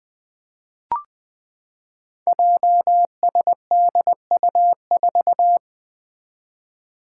Morse code, heard as a keyed tone, about 20 wpm, 700 Hz. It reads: JSDU4
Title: Five Challenging Letter-Number Combinations 20wpm